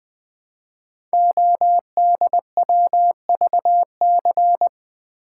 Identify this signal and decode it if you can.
Morse code, heard as a keyed tone, about 20 wpm, 700 Hz. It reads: ODWVC
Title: Code Groups - 5 Letters - 20wpm